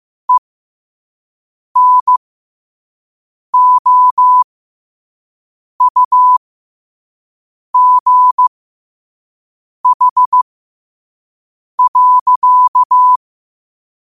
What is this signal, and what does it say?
Morse code, heard as a keyed tone, about 15 wpm, spaced as for 5 wpm, 1000 Hz. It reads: ENOUGH.
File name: qotdfiles / 5wpm